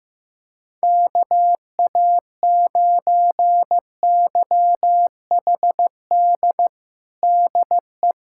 Morse code, heard as a keyed tone, about 15 wpm, 700 Hz. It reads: KA9YHD DE
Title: USA Callsign DE Callsign 15wpm